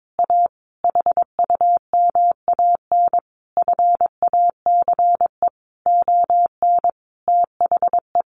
Morse code, heard as a keyed tone, about 22 wpm, 700 Hz. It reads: A HUMAN FACE ON THE
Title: Sentences from Top 1200 Words 22wpm